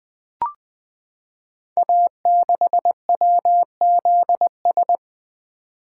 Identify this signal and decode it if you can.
Morse code, heard as a keyed tone, about 20 wpm, 700 Hz. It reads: A6WZS
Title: Code Groups - 5 Letter:Numbers - 20wpm